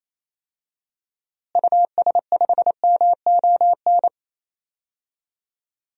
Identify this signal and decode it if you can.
Morse code, heard as a keyed tone, about 28 wpm, 700 Hz. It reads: US5MON
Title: DXCC-100 Call Signs - 6 Characters - Repeated 3x - 28wpm